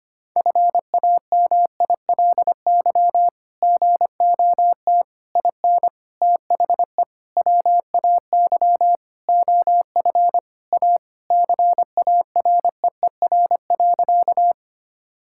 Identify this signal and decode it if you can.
Morse code, heard as a keyed tone, about 25 wpm, 700 Hz. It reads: FAMILY GOT IN THE WAY OF A CAREER.
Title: Sentences from Top 1000 Words - Review Part 6 - 25wpm